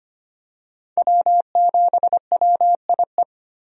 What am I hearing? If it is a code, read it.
Morse code, heard as a keyed tone, about 25 wpm, 700 Hz. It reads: W7WIE
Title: USA Call Signs - 5 Characters - 25wpm